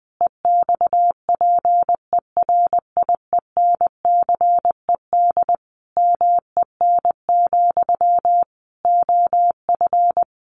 Morse code, heard as a keyed tone, about 20 wpm, 700 Hz. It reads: EXPERIENCED MEN, OF
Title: Sentences from Top 1100 Words 20wpm